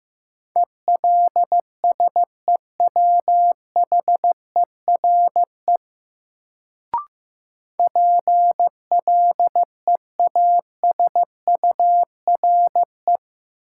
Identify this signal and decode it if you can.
Morse code, heard as a keyed tone, about 15 wpm, 700 Hz. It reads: ELSEWHERE PLEASURE
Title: Sentences from Top 1600 Words - New Word Drill - 15wpm